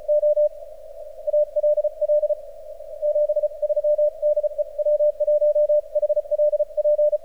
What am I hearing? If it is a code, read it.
Morse code, heard as a keyed tone, about 36 wpm, 600 Hz. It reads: O ALL 73 DE W1HLP
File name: archived / CRSnet